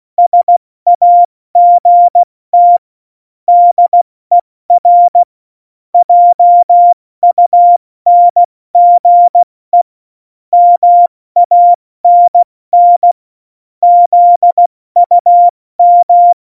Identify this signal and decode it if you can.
Morse code, heard as a keyed tone, about 16 wpm, 700 Hz. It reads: SAGT DER JUNGE MANN ZUM
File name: Morsekurs / Witze / Witz 18. Juni 2019, 70 F80